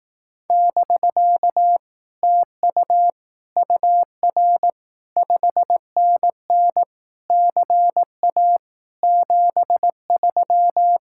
Morse code, heard as a keyed tone, about 18 wpm, 700 Hz. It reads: <BK> TU UR 5NN CA 73
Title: Speed-Racing Common QSO Phrases Encore - 15wpm